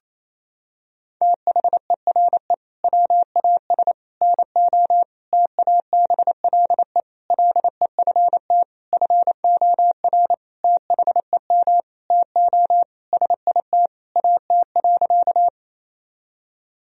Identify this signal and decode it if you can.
Morse code, heard as a keyed tone, about 28 wpm, 700 Hz. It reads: THERE WAS NO TABLE LEFT FOR THEM TO SIT AT.
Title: Sentences from Top 1000 Words - Review Part 2 - 28wpm